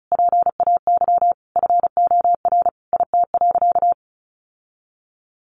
Morse code, heard as a keyed tone, about 34 wpm, 700 Hz. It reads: PAY FOR IT.
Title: Sentences from Top 1300 Words 35wpm